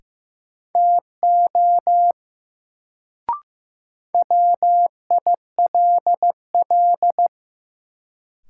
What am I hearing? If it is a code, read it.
Morse code, heard as a keyed tone, about 15 wpm, 700 Hz. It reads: TO WILL
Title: Sentences from Top 100 Words - New Word Drill - 15wpm